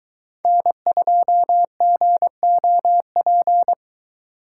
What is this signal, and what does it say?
Morse code, heard as a keyed tone, about 21 wpm, 700 Hz. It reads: N2GOP